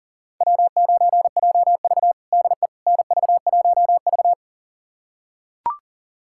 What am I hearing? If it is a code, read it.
Morse code, heard as a keyed tone, about 40 wpm, 700 Hz. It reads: W9JV DE NV1V
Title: USA Callsign DE Callsign - 4 Characters - 40wpm